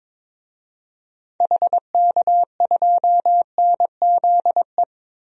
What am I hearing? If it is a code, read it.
Morse code, heard as a keyed tone, about 22 wpm, 700 Hz. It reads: HK2NZE
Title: DXCC-100 Call Signs - 6 Characters - 22wpm